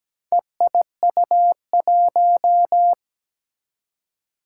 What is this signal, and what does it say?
Morse code, heard as a keyed tone, about 17 wpm, 700 Hz. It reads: EIU1
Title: Four Challenging Letter-Number Combinations 17wpm